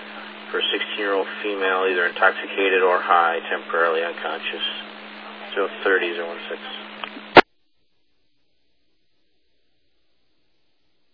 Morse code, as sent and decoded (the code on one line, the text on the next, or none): none